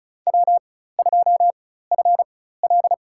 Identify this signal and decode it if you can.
Morse code, heard as a keyed tone, about 34 wpm, spaced as for 16 wpm, 700 Hz. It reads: W2FL
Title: USA Call Signs - 4 Characters - 4x Character Spacing - 35wpm